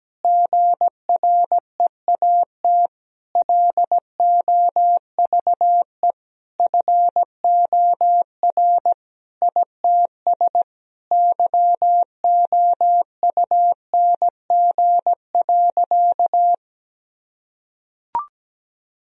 Morse code, heard as a keyed tone, about 17 wpm, 700 Hz. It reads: GREAT LOVE FOR ITS YOUNG.
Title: Sentences from Top 1000 Words - Review Part 2 - 17wpm